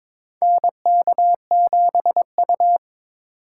Morse code, heard as a keyed tone, about 22 wpm, 700 Hz. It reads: NK7U
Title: K1USN SST QSO Practice - 22wpm